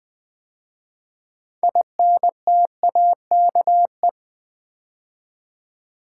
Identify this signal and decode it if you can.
Morse code, heard as a keyed tone, about 20 wpm, 700 Hz. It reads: INTAKE
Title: 6 Letter Words - First 3 Letters Given - 20wpm